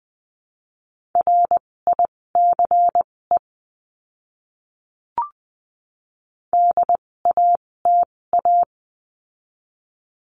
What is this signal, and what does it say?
Morse code, heard as a keyed tone, about 20 wpm, spaced as for 16 wpm, 700 Hz. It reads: RICE DATA
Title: 256 Lesson 42a - Words 12-20wpm